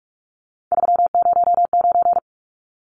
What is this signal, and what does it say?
Morse code, heard as a keyed tone, about 44 wpm, 700 Hz. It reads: 309
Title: Numbers - 3-Digit 45wpm